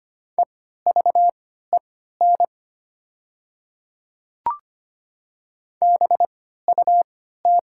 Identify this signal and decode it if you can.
Morse code, heard as a keyed tone, about 25 wpm, spaced as for 14 wpm, 690 Hz. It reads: EVEN BUT
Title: 3x Character Spacing - Top 200 Words - 25wpm